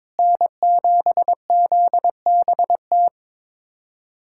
Morse code, heard as a keyed tone, about 21 wpm, 700 Hz. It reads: N7ZBT
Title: USA Callsign DE Callsign 22wpm